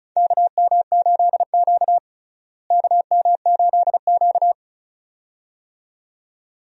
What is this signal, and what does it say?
Morse code, heard as a keyed tone, about 36 wpm, 700 Hz. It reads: KM8Q KM8Q
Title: USA Call Signs - 4 Characters - Repeated 2x - 35wpm